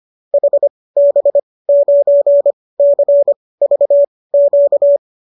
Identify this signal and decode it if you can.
Morse code, heard as a keyed tone, about 25 wpm, spaced as for 18 wpm, 570 Hz. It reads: HB9CVQ